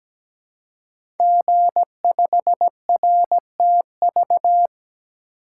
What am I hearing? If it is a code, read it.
Morse code, heard as a keyed tone, about 17 wpm, 700 Hz. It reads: G5RTV
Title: Code Groups - 5 Letter:Numbers - 17wpm